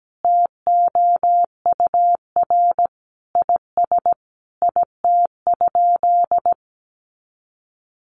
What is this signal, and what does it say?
Morse code, heard as a keyed tone, about 17 wpm, 700 Hz. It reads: TOUR IS IT?
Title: Sentences from Top 1400 Words 17wpm